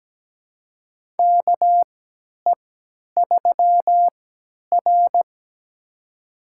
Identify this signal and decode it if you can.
Morse code, heard as a keyed tone, about 17 wpm, spaced as for 10 wpm, 700 Hz. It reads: KE3R
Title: USA Call Signs - 4 Characters - 3x Character Spacing - 17wpm